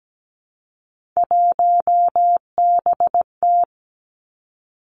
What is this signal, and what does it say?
Morse code, heard as a keyed tone, about 17 wpm, 700 Hz. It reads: 1BT
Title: Three Challenging Letter-Number Combinations 17wpm